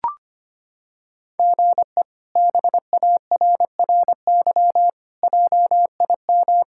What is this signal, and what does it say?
Morse code, heard as a keyed tone, about 25 wpm, 700 Hz. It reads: GE BARRY JIM
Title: K1USN SST QSO Practice - 25wpm